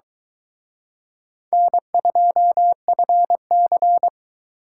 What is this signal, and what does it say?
Morse code, heard as a keyed tone, about 23 wpm, 700 Hz. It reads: N2FC